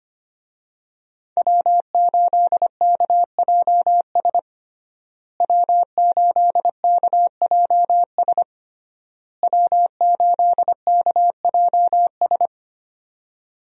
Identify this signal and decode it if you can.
Morse code, heard as a keyed tone, about 25 wpm, 700 Hz. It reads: W8KJS W8KJS W8KJS
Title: USA Call Signs - 5 Characters - Repeated 3x - 25wpm